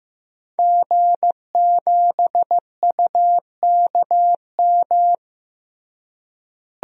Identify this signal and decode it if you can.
Morse code, heard as a keyed tone, about 16 wpm, 700 Hz. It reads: G7UKM
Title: DXCC-100 Call Signs - 5 Characters - Repeated 2x - 15wpm